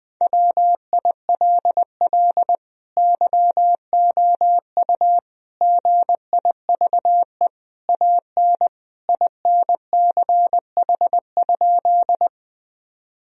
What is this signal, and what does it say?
Morse code, heard as a keyed tone, about 20 wpm, 700 Hz. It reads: WILL YOU GIVE AN INCH?
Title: Sentences from Top 1000 Words - Review Part 3 - 20wpm